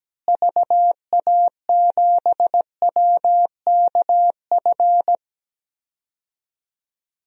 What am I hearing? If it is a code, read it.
Morse code, heard as a keyed tone, about 17 wpm, 700 Hz. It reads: VA7WKF
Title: DXCC-100 Call Signs - 6 Characters - Repeated 3x - 17wpm